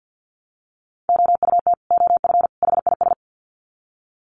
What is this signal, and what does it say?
Morse code, heard as a keyed tone, about 50 wpm, 700 Hz. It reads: OUT OF HIS
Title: Sets of 3 Words 50wpm